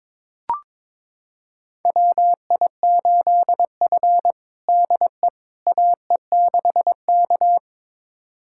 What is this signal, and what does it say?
Morse code, heard as a keyed tone, about 22 wpm, 700 Hz. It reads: WI8F DE AE6K
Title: USA Callsign DE Callsign - 4 Characters - 22wpm